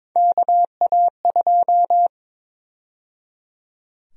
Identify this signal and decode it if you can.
Morse code, heard as a keyed tone, about 21 wpm, 700 Hz. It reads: KA2
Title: USA Call Sign Prefixes - 22wpm